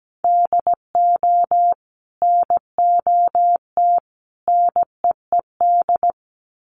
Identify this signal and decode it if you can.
Morse code, heard as a keyed tone, about 17 wpm, 700 Hz. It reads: DO NOT NEED
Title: Sets of 3 Words 17wpm